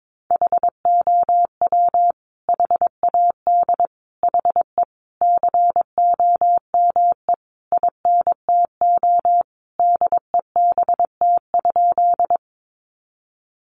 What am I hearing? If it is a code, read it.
Morse code, heard as a keyed tone, about 22 wpm, 700 Hz. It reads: HOW HAD HE COME INTO DEBT?
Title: Sentences from Top 1300 Words 22wpm